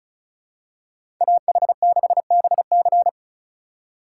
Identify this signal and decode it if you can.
Morse code, heard as a keyed tone, about 34 wpm, 700 Hz. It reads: AH6BC
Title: USA Call Signs - 8x Character Spacing - 35wpm